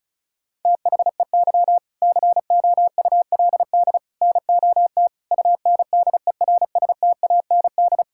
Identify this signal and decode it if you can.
Morse code, heard as a keyed tone, about 36 wpm, 700 Hz. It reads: THEY COULD NOT UNDERSTAND
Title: Sentences from Top 1000 Words - Review Part 6 - 35wpm